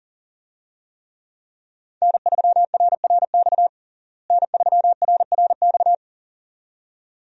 Morse code, heard as a keyed tone, about 40 wpm, 700 Hz. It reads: N3RRX N3RRX
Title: USA Call Signs - 5 Characters - Repeated 2x - 40wpm